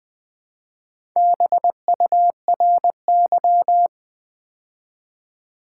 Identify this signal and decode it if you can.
Morse code, heard as a keyed tone, about 20 wpm, 700 Hz. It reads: BURY
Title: Build A Word - 2 Letters At A Time - 4-Letter Words - 20wpm